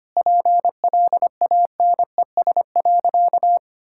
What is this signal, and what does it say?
Morse code, heard as a keyed tone, about 25 wpm, 700 Hz. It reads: PLANES.
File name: Easy Sentences with Hints - 25wpm